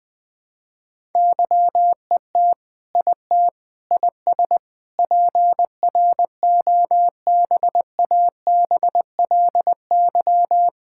Morse code, heard as a keyed tone, about 20 wpm, 700 Hz. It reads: YET IT IS PROBABLY
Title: Sentences from Top 1600 Words 20wpm